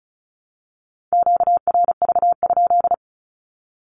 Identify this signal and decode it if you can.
Morse code, heard as a keyed tone, about 34 wpm, 700 Hz. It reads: QRV?
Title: Q-Codes 35wpm